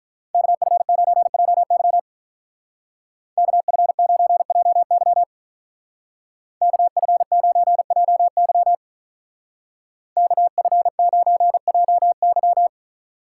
Morse code, heard as a keyed tone, about 43 wpm, 700 Hz. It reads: KF9JY KF9JY KF9JY KF9JY
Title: Speed Racing - USA 2x2 Call Signs 35wpm